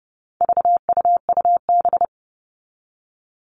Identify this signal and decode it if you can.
Morse code, heard as a keyed tone, about 30 wpm, 700 Hz. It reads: VUUB
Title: All Cut Numbers - 4-Digit 30wpm